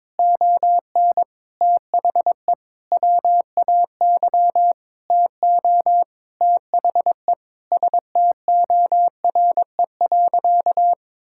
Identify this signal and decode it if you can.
Morse code, heard as a keyed tone, about 22 wpm, 700 Hz. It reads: ON THE WAY TO THE STORE.
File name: practice / 22wpm / Sentences with 8-Word Phrases - 22wpm